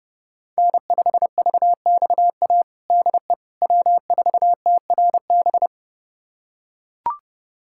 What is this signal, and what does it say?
Morse code, heard as a keyed tone, about 32 wpm, 700 Hz. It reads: N5VXA DE W4TRB
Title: USA Callsign DE Callsign - 5 Characters - 30wpm